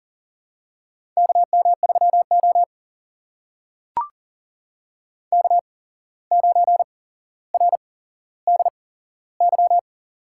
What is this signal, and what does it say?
Morse code, heard as a keyed tone, about 38 wpm, spaced as for 16 wpm, 700 Hz. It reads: KM3O K9RDY
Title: USA Call Signs - 8x Character Spacing - 40wpm